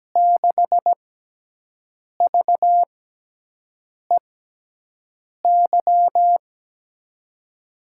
Morse code, heard as a keyed tone, about 17 wpm, spaced as for 6 wpm, 700 Hz. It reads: 6VEY
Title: USA Call Signs - 6 Characters - 6x Character Spacing - 17wpm